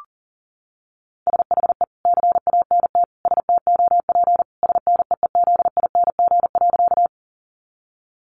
Morse code, heard as a keyed tone, about 40 wpm, 700 Hz. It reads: SHE CANT STOP SNEEZING.
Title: Easy Phrases 40wpm